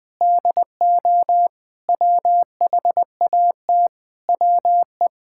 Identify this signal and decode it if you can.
Morse code, heard as a keyed tone, about 21 wpm, 700 Hz. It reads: DO WHAT WE
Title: Sentences from Top 1600 Words 20wpm